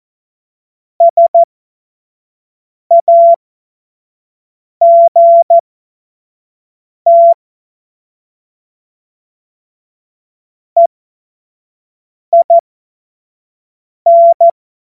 Morse code, heard as a keyed tone, about 14 wpm, spaced as for 5 wpm, 680 Hz. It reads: SAGT EIN
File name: Morsekurs / Witze / Witz 09. Dezember 2025 25 F70